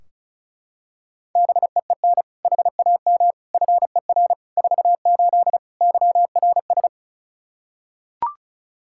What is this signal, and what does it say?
Morse code, heard as a keyed tone, about 36 wpm, 700 Hz. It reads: BEEN HAM FER 48 YRS
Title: Practice QSO - Exchange 3 - More Info - 35wpm